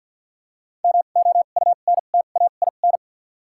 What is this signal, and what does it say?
Morse code, heard as a keyed tone, about 53 wpm, 700 Hz. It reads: M O U N T A I N
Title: 2x Character Spacing - Top 300 Words - 50wpm